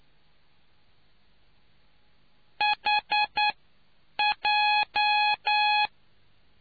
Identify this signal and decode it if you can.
Morse code, heard as a keyed tone, about 9 wpm, spaced as for 7 wpm, 840 Hz. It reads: HJ